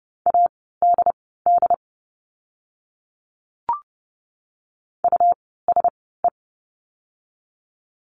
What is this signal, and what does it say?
Morse code, heard as a keyed tone, about 30 wpm, spaced as for 17 wpm, 700 Hz. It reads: ADD USE
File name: Lesson 17f - Letter u - Words 30wpm